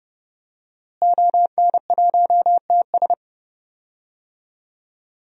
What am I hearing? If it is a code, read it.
Morse code, heard as a keyed tone, about 30 wpm, 700 Hz. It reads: ON1TS
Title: DXCC-100 Call Signs - 5 Characters - Repeated 2x - 30wpm